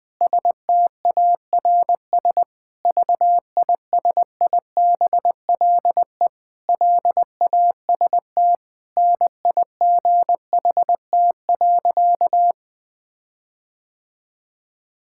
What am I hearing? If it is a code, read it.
Morse code, heard as a keyed tone, about 20 wpm, 700 Hz. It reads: STARS VISIBLE LAST NIGHT.
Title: Sentences with 5-Word Phrases - 20wpm